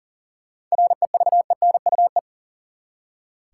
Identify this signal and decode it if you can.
Morse code, heard as a keyed tone, about 40 wpm, 700 Hz. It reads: REVENUE
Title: Seven Letter Words - 40wpm